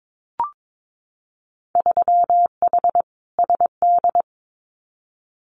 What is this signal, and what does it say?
Morse code, heard as a keyed tone, about 22 wpm, 700 Hz. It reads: 3H SD
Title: Winter Field Day Exchanges Encore 22wpm